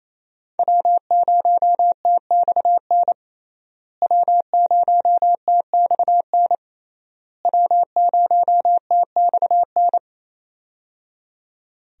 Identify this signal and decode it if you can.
Morse code, heard as a keyed tone, about 28 wpm, 700 Hz. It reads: W0TXN W0TXN W0TXN
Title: USA Call Signs - 5 Characters - Repeated 3x - 28wpm